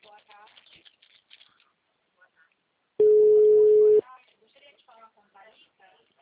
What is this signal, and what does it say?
Ringback tone (Europe, 425 Hz), with speech.